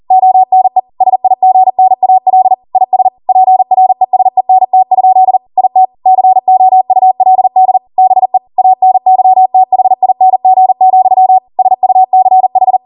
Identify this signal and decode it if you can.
Morse code, heard as a keyed tone, about 40 wpm, 750 Hz. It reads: ONE SIGNAL IS PRESENT? IT COULD BE ANYTHING, SUCH
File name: Morse / 40WPM